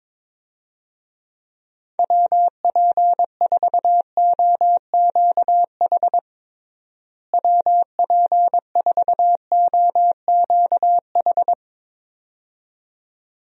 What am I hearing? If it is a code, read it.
Morse code, heard as a keyed tone, about 21 wpm, 700 Hz. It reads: WP4OQH WP4OQH
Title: USA Call Signs - 6 Characters - Repeated 2x - 22wpm